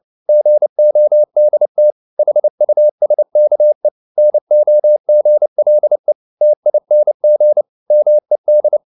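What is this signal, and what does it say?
Morse code, heard as a keyed tone, about 29 wpm, 600 Hz. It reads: GODT HUSKE NOGLE TING MED
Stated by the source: Fast (QRQ) Morse at 29 wpm, from CW med Gnister.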